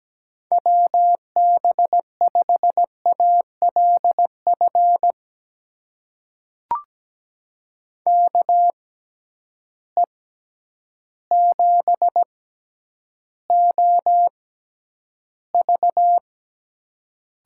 Morse code, heard as a keyed tone, about 17 wpm, spaced as for 9 wpm, 700 Hz. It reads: WB5ALF KE7OV
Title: USA Call Signs - 6 Characters - 6x Character Spacing - 17wpm